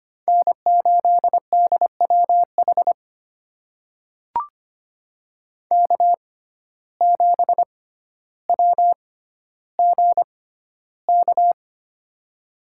Morse code, heard as a keyed tone, about 25 wpm, spaced as for 13 wpm, 700 Hz. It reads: N8DWH K7WGK
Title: USA Call Signs - 5 Characters - 6x Character Spacing - 25wpm